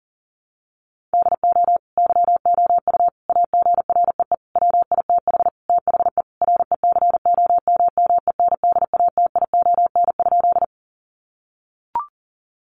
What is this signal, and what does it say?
Morse code, heard as a keyed tone, about 40 wpm, 700 Hz. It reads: DO YOU AGREE WITH THE RECOMMENDATION?
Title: Sentences from Top 1400 Words 40wpm